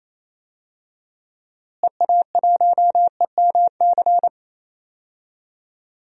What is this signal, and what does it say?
Morse code, heard as a keyed tone, about 28 wpm, 700 Hz. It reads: EA1EMC
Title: DXCC-100 Call Signs - 6 Characters - Repeated 3x - 28wpm